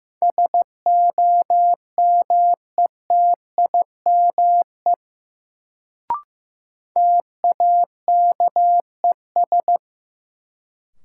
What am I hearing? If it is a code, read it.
Morse code, heard as a keyed tone, about 16 wpm, 700 Hz. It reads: SOMETIME TAKES
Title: Sentences from Top 1000 Words - Review Part 6 - New Word Drill - 15wpm